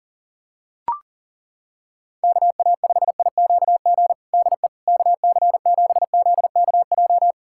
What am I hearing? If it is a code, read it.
Morse code, heard as a keyed tone, about 40 wpm, 700 Hz. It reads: KA5IQG DE KC7ZKJ
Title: USA Callsign DE Callsign - 6 Characters - 40wpm